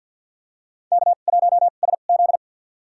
Morse code, heard as a keyed tone, about 52 wpm, spaced as for 37 wpm, 700 Hz. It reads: K1SZ
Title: USA Call Signs - 4 Characters - 2x Character Spacing - 50wpm